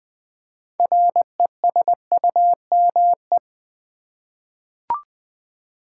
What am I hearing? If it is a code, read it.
Morse code, heard as a keyed tone, about 20 wpm, 700 Hz. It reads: RESUME
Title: 6 Letter Words - First 3 Letters Given - 20wpm